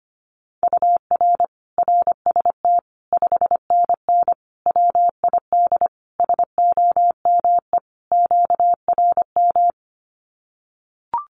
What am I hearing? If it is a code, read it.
Morse code, heard as a keyed tone, about 25 wpm, 700 Hz. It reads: UR RST 5NN WID SOME QRM
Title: Common QSO Phrases 25wpm